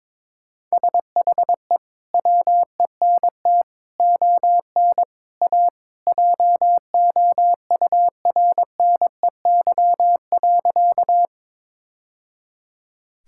Morse code, heard as a keyed tone, about 22 wpm, 700 Hz. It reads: SHE WENT ON A JOURNEY.
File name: practice / 22wpm / Sentences with 2-Word Phrases - 22wpm